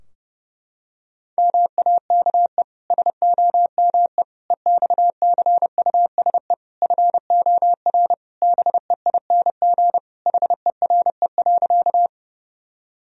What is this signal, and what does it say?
Morse code, heard as a keyed tone, about 32 wpm, 700 Hz. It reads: MAKE SOME EXCUSE FOR BEING HERE.
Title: Sentences from Top 1000 Words - Review Part 3 - 30wpm